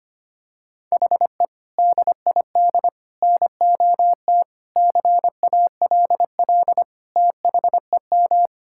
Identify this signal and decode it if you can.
Morse code, heard as a keyed tone, about 25 wpm, 700 Hz. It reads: HE DID NOT CALL THEM
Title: Sentences from Top 1600 Words 25wpm